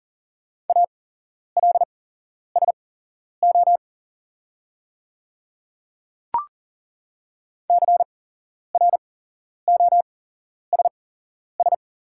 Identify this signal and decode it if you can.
Morse code, heard as a keyed tone, about 40 wpm, spaced as for 11 wpm, 690 Hz. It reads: ALSO CROSS
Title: 8x Character Spacing - Top 300 Words - 40wpm